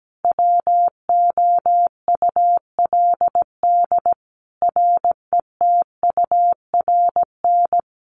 Morse code, heard as a keyed tone, about 17 wpm, 700 Hz. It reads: WOULD RETURN
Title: Sentences from Top 1400 Words 17wpm